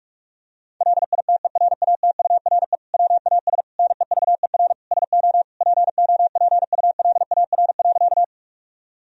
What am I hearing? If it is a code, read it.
Morse code, heard as a keyed tone, about 46 wpm, 700 Hz. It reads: LITERATURE WAS NEVER SO POPULAR.
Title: Sentences from Top 1000 Words - Review Part 3 - 45wpm